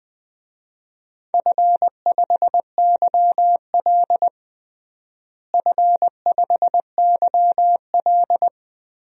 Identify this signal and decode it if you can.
Morse code, heard as a keyed tone, about 20 wpm, 700 Hz. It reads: F5YL F5YL
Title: DXCC-100 Call Signs - 4 Characters - Repeated 2x - 20wpm